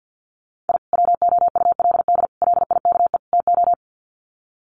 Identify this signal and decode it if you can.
Morse code, heard as a keyed tone, about 50 wpm, 700 Hz. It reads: I WOULD LIKE TO
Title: Sets of 4 Words 50wpm